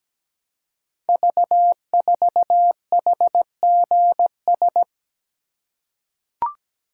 Morse code, heard as a keyed tone, about 17 wpm, 700 Hz. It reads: V4HGS
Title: Code Groups - 5 Letter:Numbers - 17wpm